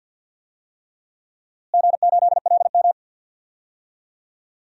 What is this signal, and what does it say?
Morse code, heard as a keyed tone, about 46 wpm, 700 Hz. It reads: G8LM
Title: DXCC-100 Call Signs - 4 Characters - Repeated 3x - 50wpm